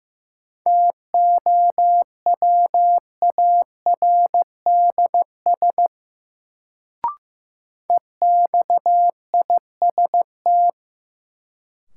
Morse code, heard as a keyed tone, about 15 wpm, 700 Hz. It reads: TOWARDS EXIST
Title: Sentences from Top 1100 Words - New Word Drill Alt - 15wpm